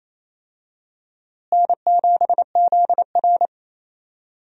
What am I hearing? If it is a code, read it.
Morse code, heard as a keyed tone, about 28 wpm, 700 Hz. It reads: N7ZR
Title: USA Call Signs - 4 Characters - Repeated 2x - 28wpm